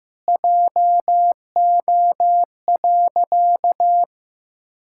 Morse code, heard as a keyed tone, about 15 wpm, 700 Hz. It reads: JO.
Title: Pangram Challenge v2 - 15wpm